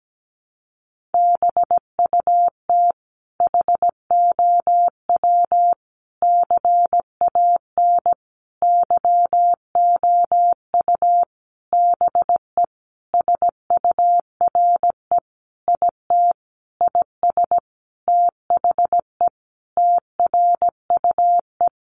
Morse code, heard as a keyed tone, about 17 wpm, 700 Hz. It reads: BUT HOW CAN YOU BE SURE IT IS THE TRUE
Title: Sentences from Top 1400 Words 17wpm